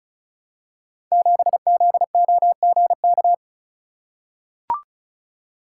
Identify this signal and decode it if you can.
Morse code, heard as a keyed tone, about 36 wpm, 700 Hz. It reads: 7ZOGK
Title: Code Groups - 5 Letter:Numbers - 35wpm